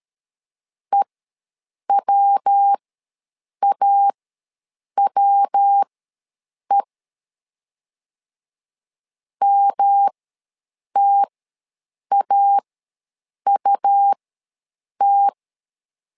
Morse code, heard as a keyed tone, about 13 wpm, spaced as for 7 wpm, 800 Hz. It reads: EWAWE MTAUT
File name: LEZIONI / 3 MWRO / 3 MW ws 12wpm 1816